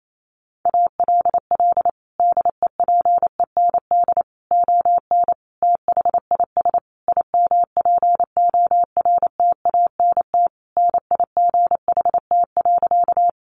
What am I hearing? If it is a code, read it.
Morse code, heard as a keyed tone, about 28 wpm, 700 Hz. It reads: ALL DEPEND ON THIS IMPORTANT NIGHT.
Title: Sentences from Top 1300 Words 28wpm